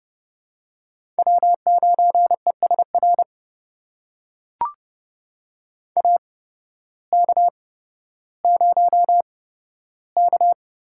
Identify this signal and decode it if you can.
Morse code, heard as a keyed tone, about 30 wpm, spaced as for 14 wpm, 700 Hz. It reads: W9ESR AK0K
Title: USA Call Signs - 8x Character Spacing - 30wpm